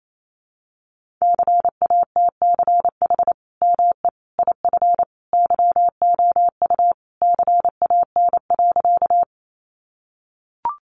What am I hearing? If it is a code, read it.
Morse code, heard as a keyed tone, about 28 wpm, 700 Hz. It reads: CATCH ME IF YOU CAN.
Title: Sentences from Top 1500 Words 28wpm